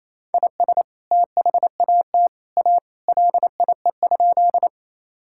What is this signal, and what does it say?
Morse code, heard as a keyed tone, about 27 wpm, 700 Hz. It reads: IS THAT A LIE?
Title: Sentences from Top 1600 Words 28wpm